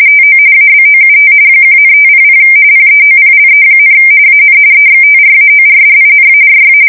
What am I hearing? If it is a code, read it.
RTTY, about 45.45 baud. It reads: TO ABOUT 1,500 PEOPLE, AN EXTRAORDINARY
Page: 170 Shift 2125/2295